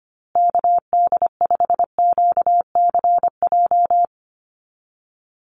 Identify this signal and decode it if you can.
Morse code, heard as a keyed tone, about 25 wpm, 700 Hz. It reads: KD5QCJ
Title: USA Call Signs Repeated 3x 25wpm